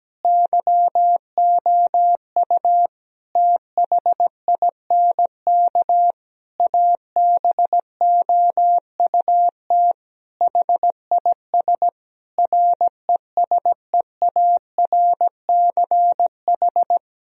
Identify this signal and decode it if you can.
Morse code, heard as a keyed tone, about 17 wpm, 700 Hz. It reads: YOU THINK ABOUT HIS RESEARCH
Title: Sentences from Top 1000 Words - Review Part 4 - 17wpm